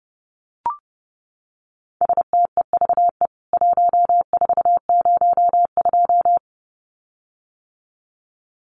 Morse code, heard as a keyed tone, about 30 wpm, 700 Hz. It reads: STEVE 1402
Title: CWT Exchange 30wpm